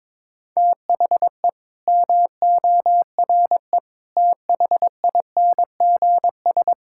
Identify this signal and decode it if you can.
Morse code, heard as a keyed tone, about 22 wpm, 700 Hz. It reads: THE MORE THINGS
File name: Sentences with 9-Word Phrases - 22wpm